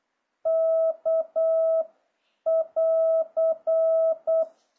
Morse code, heard as a keyed tone, about 8 wpm, 650 Hz. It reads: K <AR>